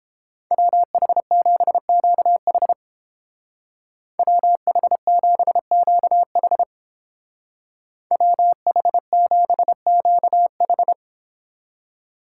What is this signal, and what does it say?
Morse code, heard as a keyed tone, about 29 wpm, 700 Hz. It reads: WH7QH WH7QH WH7QH